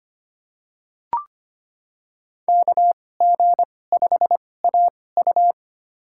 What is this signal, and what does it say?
Morse code, heard as a keyed tone, about 25 wpm, spaced as for 18 wpm, 700 Hz. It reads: KG5AU
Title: USA Call Signs - 5 Characters - 2x Character Spacing - 25wpm